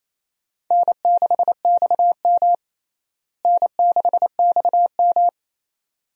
Morse code, heard as a keyed tone, about 27 wpm, 700 Hz. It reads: N6XM N6XM
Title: USA Call Signs - 4 Characters - Repeated 2x - 28wpm